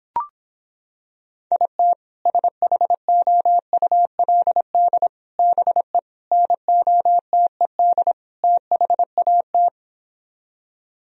Morse code, heard as a keyed tone, about 26 wpm, 700 Hz. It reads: IT SHOULD BE NOTED THAT